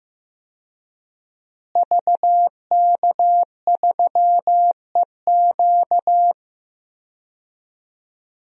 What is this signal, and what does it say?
Morse code, heard as a keyed tone, about 15 wpm, 700 Hz. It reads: VK3EQ
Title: DXCC-100 Call Signs - 5 Characters - Repeated 3x - 15wpm